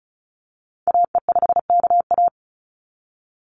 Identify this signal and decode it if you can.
Morse code, heard as a keyed tone, about 34 wpm, 700 Hz. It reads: AE5KA
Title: NA SOTA Activator Callsigns 35wpm